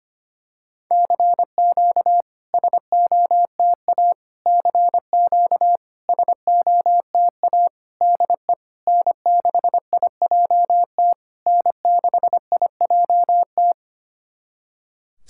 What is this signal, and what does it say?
Morse code, heard as a keyed tone, about 25 wpm, 700 Hz. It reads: CQ SOTA CQ SOTA DE N6IJT N6IJT
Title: SOTA QSO Practice 25wpm